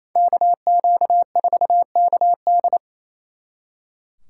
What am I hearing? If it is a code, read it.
Morse code, heard as a keyed tone, about 37 wpm, 700 Hz. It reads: KQ4KD